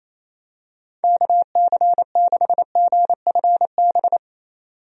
Morse code, heard as a keyed tone, about 28 wpm, 700 Hz. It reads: KC6GFB
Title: USA Call Signs - 6 Characters - 28wpm